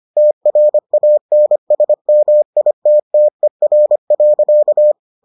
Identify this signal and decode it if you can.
Morse code, heard as a keyed tone, about 25 wpm, 600 Hz. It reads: TRANSMITTER.